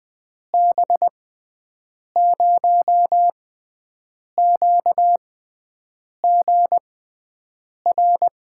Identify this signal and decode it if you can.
Morse code, heard as a keyed tone, about 20 wpm, spaced as for 7 wpm, 700 Hz. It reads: B0QGR
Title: USA Call Signs - 6x Character Spacing - 20wpm